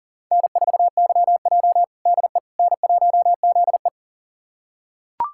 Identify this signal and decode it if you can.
Morse code, heard as a keyed tone, about 40 wpm, 700 Hz. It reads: N4YJ DE N1ZE
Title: USA Callsign DE Callsign - 4 Characters - 40wpm